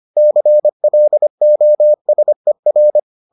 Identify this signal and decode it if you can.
Morse code, heard as a keyed tone, about 25 wpm, 600 Hz. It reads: CLOSER